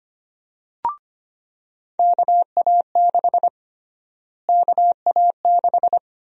Morse code, heard as a keyed tone, about 25 wpm, 700 Hz. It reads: KA6 KA6
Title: USA Call Sign Prefixes - Repeated 2x - 25wpm